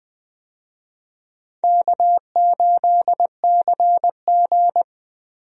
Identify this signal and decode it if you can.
Morse code, heard as a keyed tone, about 20 wpm, 700 Hz. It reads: K8CG
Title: USA Call Signs - 4 Characters - Repeated 3x - 20wpm